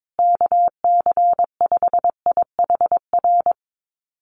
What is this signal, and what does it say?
Morse code, heard as a keyed tone, about 22 wpm, 700 Hz. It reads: KC5IHR
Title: USA Call Signs Repeated 3x 22wpm